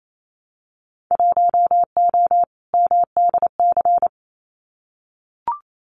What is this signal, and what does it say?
Morse code, heard as a keyed tone, about 28 wpm, 700 Hz. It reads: 1O MDC
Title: Winter Field Day Exchanges 28wpm